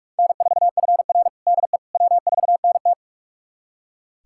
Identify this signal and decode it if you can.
Morse code, heard as a keyed tone, about 42 wpm, 700 Hz. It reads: N4FR DE W4NT
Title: USA Callsign DE Callsign - 4 Characters - 45wpm